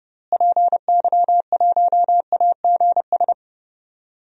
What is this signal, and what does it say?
Morse code, heard as a keyed tone, about 30 wpm, 700 Hz. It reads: PY1AGS
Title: DXCC-100 Call Signs - 6 Characters - Repeated 2x - 30wpm